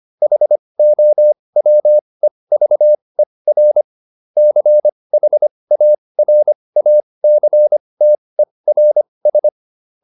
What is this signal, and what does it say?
Morse code, heard as a keyed tone, about 25 wpm, spaced as for 20 wpm, 600 Hz. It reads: HOWEVER CHARACTERS